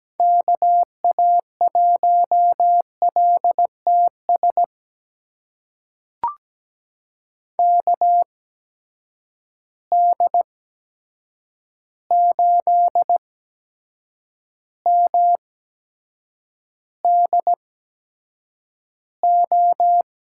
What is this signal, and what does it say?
Morse code, heard as a keyed tone, about 17 wpm, spaced as for 7 wpm, 700 Hz. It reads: KA1LTS KD8MDO
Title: USA Call Signs - 6 Characters - 8x Character Spacing - 17wpm